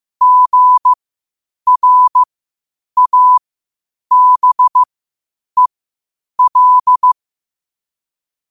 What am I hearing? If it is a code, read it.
Morse code, heard as a keyed tone, about 15 wpm, spaced as for 8 wpm, 1000 Hz. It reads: GRABEL
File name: qotdfiles / 8wpm